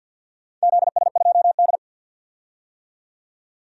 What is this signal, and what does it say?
Morse code, heard as a keyed tone, about 51 wpm, 700 Hz. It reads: ZS2D
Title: DXCC-100 Call Signs - 4 Characters - 50wpm